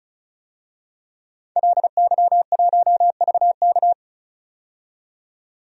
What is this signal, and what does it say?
Morse code, heard as a keyed tone, about 36 wpm, 700 Hz. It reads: LY1VK
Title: DXCC-100 Call Signs - 5 Characters - 35wpm